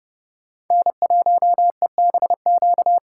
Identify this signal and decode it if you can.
Morse code, heard as a keyed tone, about 29 wpm, 700 Hz. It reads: N1EBQ